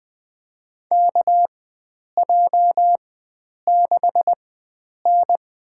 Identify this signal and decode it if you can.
Morse code, heard as a keyed tone, about 20 wpm, spaced as for 9 wpm, 700 Hz. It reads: KJ6N
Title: USA Call Signs - 4x Character Spacing - 20wpm